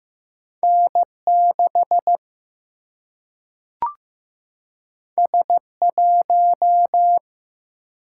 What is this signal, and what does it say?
Morse code, heard as a keyed tone, about 15 wpm, 700 Hz. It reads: N 6 S 1
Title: Single-Letter Single-Number 15wpm